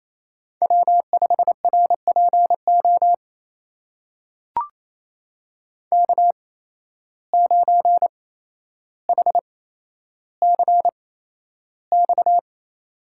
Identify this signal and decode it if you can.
Morse code, heard as a keyed tone, about 28 wpm, spaced as for 12 wpm, 700 Hz. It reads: W5RPO K9HCX
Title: USA Call Signs - 5 Characters - 8x Character Spacing - 28wpm